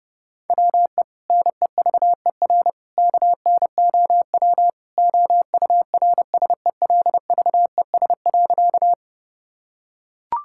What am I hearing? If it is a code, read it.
Morse code, heard as a keyed tone, about 30 wpm, 700 Hz. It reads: WE NEVER KNOW OURSELVES.
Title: Sentences from Top 1600 Words 30wpm